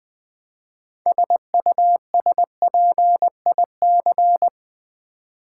Morse code, heard as a keyed tone, about 20 wpm, 700 Hz. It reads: SUSPIC
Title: Build A Word - 3 Letters At A Time - 9-Letter Words - 20wpm